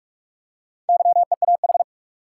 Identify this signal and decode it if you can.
Morse code, heard as a keyed tone, about 46 wpm, 700 Hz. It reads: YEAH
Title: Four Letter Words - 45wpm